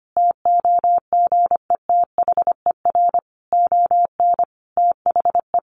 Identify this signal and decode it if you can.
Morse code, heard as a keyed tone, about 25 wpm, 700 Hz. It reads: TOGETHER ON THE
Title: Sentences from Top 1300 Words 25wpm